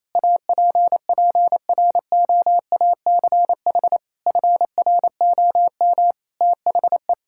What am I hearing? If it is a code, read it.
Morse code, heard as a keyed tone, about 28 wpm, 700 Hz. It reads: APPROACH FROM THE
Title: Sentences from Top 1000 Words - Review Part 6 - 28wpm